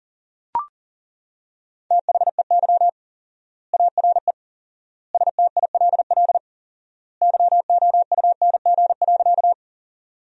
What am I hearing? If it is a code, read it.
Morse code, heard as a keyed tone, about 40 wpm, 700 Hz. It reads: THEY ARE STILL YOUNG.
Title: 4x Word-Spacing - Sentences from Top 400 Words - 40wpm